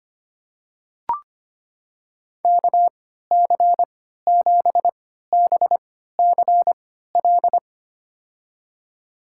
Morse code, heard as a keyed tone, about 25 wpm, spaced as for 14 wpm, 700 Hz. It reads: KC7BCL
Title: USA Call Signs - 3x Character Spacing - 25wpm